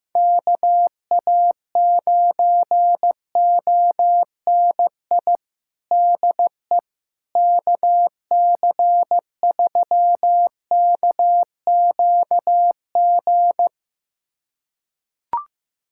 Morse code, heard as a keyed tone, about 15 wpm, 700 Hz. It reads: KA9ONI DE KC3KQG
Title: USA Callsign DE Callsign - 6 Characters - 15wpm